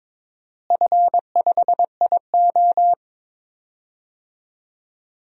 Morse code, heard as a keyed tone, about 21 wpm, 700 Hz. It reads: F5IO
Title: DXCC-100 Call Signs - 4 Characters - Repeated 3x - 22wpm